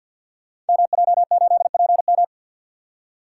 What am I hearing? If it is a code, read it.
Morse code, heard as a keyed tone, about 52 wpm, 700 Hz. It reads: MJ8PM
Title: Code Groups - 5 Letter:Numbers - 50wpm